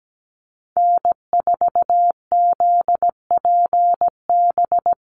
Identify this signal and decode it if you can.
Morse code, heard as a keyed tone, about 17 wpm, 700 Hz. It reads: N4ZPB
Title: USA Call Signs 17wpm